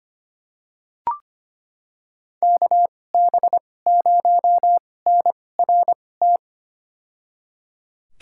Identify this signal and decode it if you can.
Morse code, heard as a keyed tone, about 25 wpm, spaced as for 18 wpm, 700 Hz. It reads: KB0NRT
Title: USA Call Signs - 6 Characters - 2x Character Spacing - 25wpm